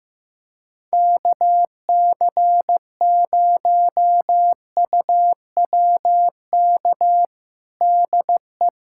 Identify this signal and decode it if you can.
Morse code, heard as a keyed tone, about 15 wpm, 700 Hz. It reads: KC0UWK DE
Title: USA Callsign DE Callsign - 6 Characters - 15wpm